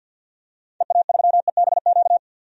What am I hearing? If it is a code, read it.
Morse code, heard as a keyed tone, about 51 wpm, 700 Hz. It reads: EA3EBQ
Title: DXCC-100 Call Signs - 6 Characters - 50wpm